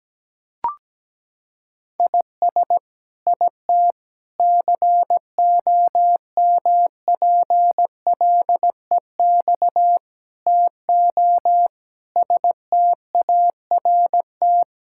Morse code, heard as a keyed tone, about 17 wpm, 700 Hz. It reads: IS IT COMPLEX TO START
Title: Sentences from Top 1000 Words - Review Part 6 - 17wpm